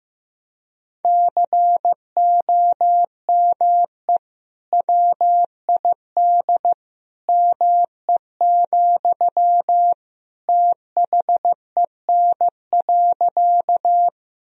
Morse code, heard as a keyed tone, about 15 wpm, 700 Hz. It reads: COME WID ME, THEN.
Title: Sentences from Top 200 Words and Abbreviations 15wpm